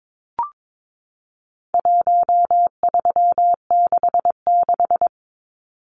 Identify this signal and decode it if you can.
Morse code, heard as a keyed tone, about 22 wpm, 700 Hz. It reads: 1366
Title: Numbers - Contest Style - 4-Digit 22wpm